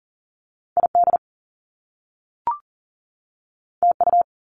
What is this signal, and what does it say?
Morse code, heard as a keyed tone, about 42 wpm, 700 Hz. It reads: ID TU
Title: Top Two Letter Combinations in Words 40wpm